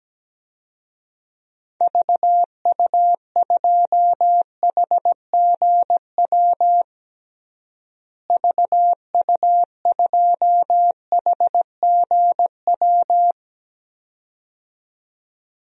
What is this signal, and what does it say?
Morse code, heard as a keyed tone, about 17 wpm, 700 Hz. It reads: VU2HGW VU2HGW
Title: DXCC-100 Call Signs - 6 Characters - Repeated 2x - 17wpm